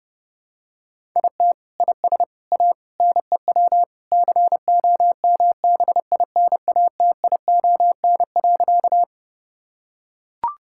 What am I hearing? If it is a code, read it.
Morse code, heard as a keyed tone, about 30 wpm, 700 Hz. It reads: IT IS A NEW COMBINATION.
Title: Sentences from Top 1000 Words - Review Part 6 - 30wpm